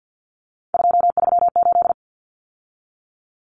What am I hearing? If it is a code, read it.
Morse code, heard as a keyed tone, about 50 wpm, 700 Hz. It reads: 238
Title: Numbers - 3-Digit 50wpm